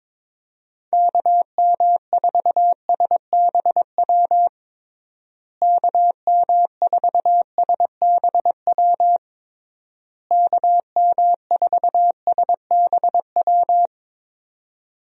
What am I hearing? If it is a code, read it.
Morse code, heard as a keyed tone, about 22 wpm, 700 Hz. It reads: KM4SBW KM4SBW KM4SBW
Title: USA Call Signs - 6 Characters - Repeated 3x - 22wpm